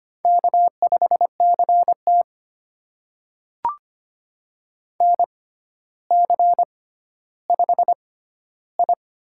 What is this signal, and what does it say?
Morse code, heard as a keyed tone, about 25 wpm, spaced as for 13 wpm, 700 Hz. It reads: K5CT NC5I
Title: USA Call Signs - 4 Characters - 6x Character Spacing - 25wpm